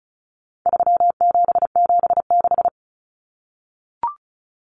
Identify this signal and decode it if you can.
Morse code, heard as a keyed tone, about 34 wpm, 700 Hz. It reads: 3776
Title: Numbers - 4-Digit 35wpm